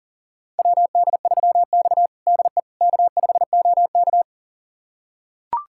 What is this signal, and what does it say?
Morse code, heard as a keyed tone, about 37 wpm, 700 Hz. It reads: WD3X DE K5OK
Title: USA Callsign DE Callsign - 4 Characters - 40wpm